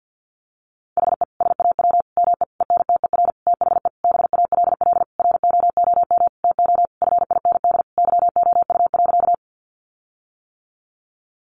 Morse code, heard as a keyed tone, about 50 wpm, 700 Hz. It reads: HE SAW ME ENTER THE BALLROOM TO FIND YOU.
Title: Sentences from Top 1300 Words 50wpm